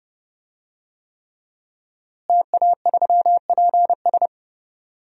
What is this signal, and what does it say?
Morse code, heard as a keyed tone, about 30 wpm, 700 Hz. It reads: TA3PS
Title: DXCC-100 Call Signs - 5 Characters - Repeated 3x - 30wpm